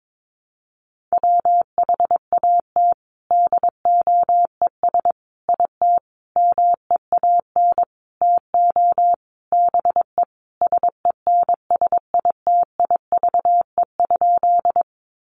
Morse code, heard as a keyed tone, about 22 wpm, 700 Hz. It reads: WHAT DOES IT MEAN TO BE SENSITIVE?
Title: Sentences from Top 1400 Words 22wpm